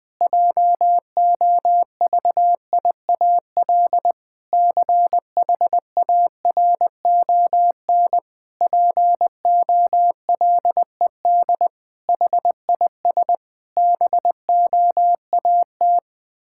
Morse code, heard as a keyed tone, about 21 wpm, 700 Hz. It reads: JOVIAL CHARON POLED HIS BOAT
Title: Pangram Challenge v2 - 20wpm